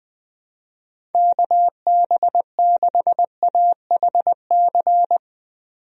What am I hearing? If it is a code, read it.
Morse code, heard as a keyed tone, about 20 wpm, 700 Hz. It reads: KB6AHC
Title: USA Call Signs - 6 Characters - 20wpm